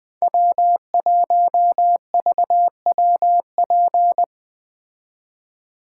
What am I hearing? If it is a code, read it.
Morse code, heard as a keyed tone, about 20 wpm, 700 Hz. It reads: W1VWP